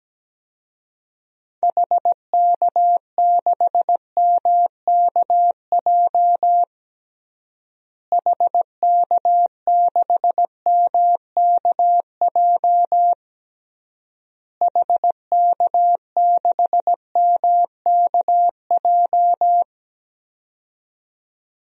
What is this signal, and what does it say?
Morse code, heard as a keyed tone, about 17 wpm, 700 Hz. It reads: HK6MKJ HK6MKJ HK6MKJ
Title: DXCC-100 Call Signs - 6 Characters - Repeated 3x - 17wpm